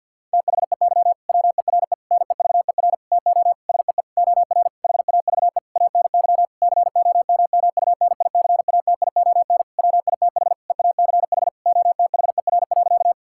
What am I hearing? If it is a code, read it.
Morse code, heard as a keyed tone, about 52 wpm, 700 Hz. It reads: THEY WERE NEVER TO SEE OR HAVE ANY COMMUNICATION WITH EACH OTHER.
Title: Sentences from Top 1000 Words - Review Part 6 - 50wpm